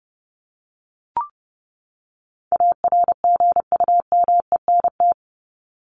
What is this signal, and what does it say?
Morse code, heard as a keyed tone, about 30 wpm, 700 Hz. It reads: ARGUMENT
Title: Top 1251-1500 Words 30wpm